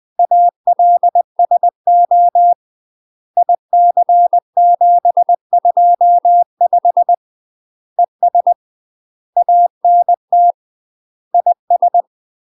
Morse code, heard as a keyed tone, about 20 wpm, 700 Hz. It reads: ALSO IC725 ES ANT IS
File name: UK MASTER QSOs TRAINING / 20WPM / UK MASTER QSOs 0067